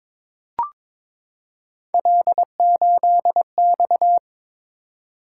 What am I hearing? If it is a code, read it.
Morse code, heard as a keyed tone, about 22 wpm, 700 Hz. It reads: L8X
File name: practice / 22wpm / Three Letter-Number Combinations - 22wpm